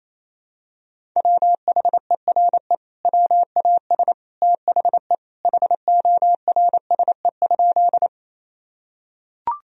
Morse code, heard as a keyed tone, about 28 wpm, 700 Hz. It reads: WHERE WAS THE HORSE?
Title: Sentences from Top 1000 Words - Review Part 1 - 28wpm